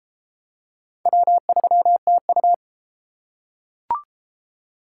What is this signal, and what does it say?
Morse code, heard as a keyed tone, about 31 wpm, 700 Hz. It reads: W3TU